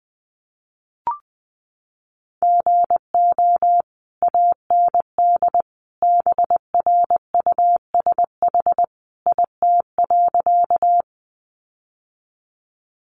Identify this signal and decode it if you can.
Morse code, heard as a keyed tone, about 20 wpm, 700 Hz. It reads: GO AND BRUSH IT.
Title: Sentences from Top 1100 Words 20wpm